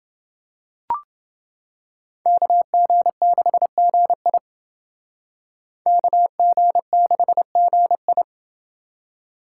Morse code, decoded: KG6GI KG6GI

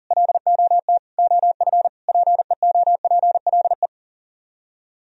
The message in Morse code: LOT OF PEOPLE

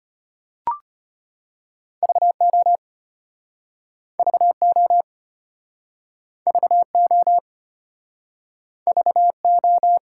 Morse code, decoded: VO VO VO VO

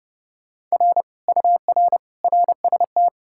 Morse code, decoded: R UR RST

